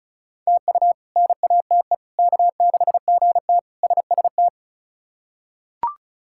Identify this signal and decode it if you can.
Morse code, heard as a keyed tone, about 34 wpm, 700 Hz. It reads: TU NATE K6GT SST
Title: K1USN SST QSO Practice - 35wpm